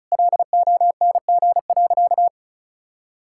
Morse code, heard as a keyed tone, about 34 wpm, 700 Hz. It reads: LONG.
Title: Sentences from Top 1000 Words - Review Part 6 - 35wpm